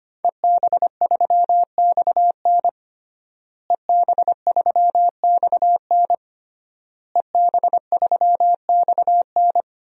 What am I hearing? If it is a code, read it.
Morse code, heard as a keyed tone, about 25 wpm, 700 Hz. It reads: EB3XN EB3XN EB3XN
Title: DXCC-100 Call Signs - 5 Characters - Repeated 3x - 25wpm